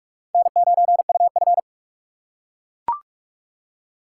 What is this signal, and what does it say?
Morse code, heard as a keyed tone, about 42 wpm, 700 Hz. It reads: N9UF
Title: USA Call Signs - 4x Character Spacing - 45wpm